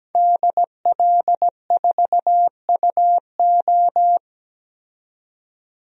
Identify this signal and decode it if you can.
Morse code, heard as a keyed tone, about 17 wpm, 700 Hz. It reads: DL4UO
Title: DXCC-100 Call Signs - 5 Characters - 17wpm